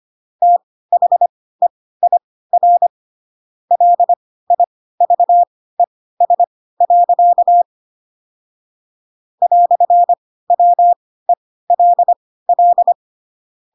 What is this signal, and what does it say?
Morse code, heard as a keyed tone, about 25 wpm, spaced as for 16 wpm, 700 Hz. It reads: THEIR LIVES. "WELL
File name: Pinocchio / Pinocchio ch2 10 25 ik1hge